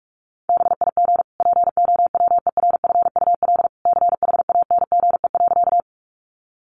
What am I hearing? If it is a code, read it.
Morse code, heard as a keyed tone, about 44 wpm, 700 Hz. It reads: BIG POWERFUL CHANGE.
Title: Sentences from Top 1300 Words 45wpm